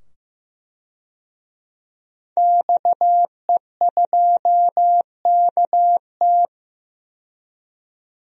Morse code, heard as a keyed tone, about 15 wpm, 700 Hz. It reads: XE2KT
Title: DXCC-100 Call Signs - 5 Characters - 15wpm